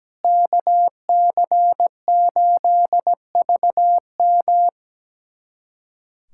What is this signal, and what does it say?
Morse code, heard as a keyed tone, about 17 wpm, 700 Hz. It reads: KC8VM
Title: USA Callsign DE Callsign - 5 Characters - 17wpm